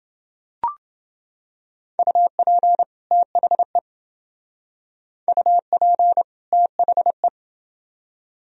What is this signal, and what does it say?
Morse code, heard as a keyed tone, about 29 wpm, 700 Hz. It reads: UP THE UP THE